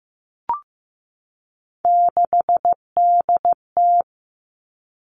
Morse code, heard as a keyed tone, about 15 wpm, 700 Hz. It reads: 6DT
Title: Three Challenging Letter-Number Combinations 15wpm